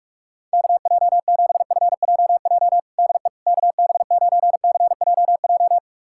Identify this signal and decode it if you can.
Morse code, heard as a keyed tone, about 45 wpm, 700 Hz. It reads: KJ7FJJ DE KB9CJJ
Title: USA Callsign DE Callsign - 6 Characters - 45wpm